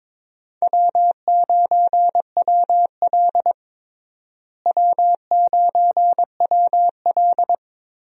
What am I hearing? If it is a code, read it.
Morse code, heard as a keyed tone, about 22 wpm, 700 Hz. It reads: W9WL W9WL
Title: USA Call Signs - 4 Characters - Repeated 3x - 22wpm